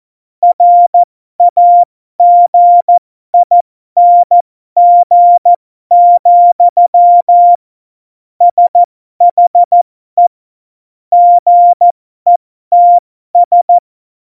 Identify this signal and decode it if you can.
Morse code, heard as a keyed tone, about 14 wpm, spaced as for 12 wpm, 700 Hz. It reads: RAGING, SHE GETS
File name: Morsekurs / Witze